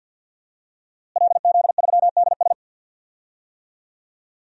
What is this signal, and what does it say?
Morse code, heard as a keyed tone, about 52 wpm, 700 Hz. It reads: LZ3DS
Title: DXCC-100 Call Signs - 5 Characters - Repeated 2x - 50wpm